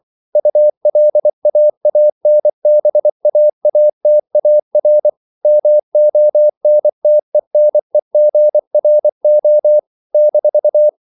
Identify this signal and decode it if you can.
Morse code, heard as a keyed tone, about 24 wpm, 600 Hz. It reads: ULAANBAATAR MONTENEGRO -
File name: CW / Practice / NationalCapitals24